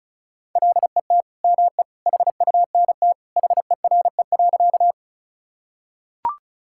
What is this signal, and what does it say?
Morse code, heard as a keyed tone, about 36 wpm, 700 Hz. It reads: LET ME HUNT HERE.
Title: Sentences from Top 1600 Words 35wpm